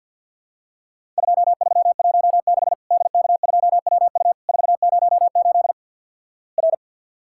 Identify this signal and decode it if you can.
Morse code, heard as a keyed tone, about 50 wpm, 700 Hz. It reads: 2316 DK2WU 408 R
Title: Worked All Europe DX Contest - QTC Exchange with Repeats - 50wpm